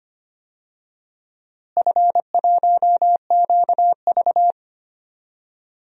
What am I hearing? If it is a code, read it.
Morse code, heard as a keyed tone, about 25 wpm, 700 Hz. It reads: F1QV
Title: DXCC-100 Call Signs - 4 Characters - Repeated 2x - 25wpm